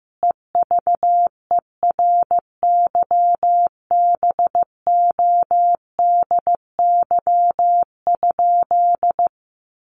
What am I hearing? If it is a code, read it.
Morse code, heard as a keyed tone, about 15 wpm, 700 Hz. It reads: EVERYBODY?
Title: Sentences from Top 1400 Words 15wpm